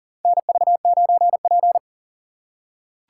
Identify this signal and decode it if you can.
Morse code, heard as a keyed tone, about 40 wpm, 700 Hz. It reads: NV9P
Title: USA Call Signs - 3x Character Spacing - 40wpm